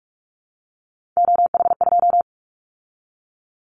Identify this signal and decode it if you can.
Morse code, heard as a keyed tone, about 46 wpm, 690 Hz. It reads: OH2
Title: Callsign Prefixes for Top 50 DXCC - 45wpm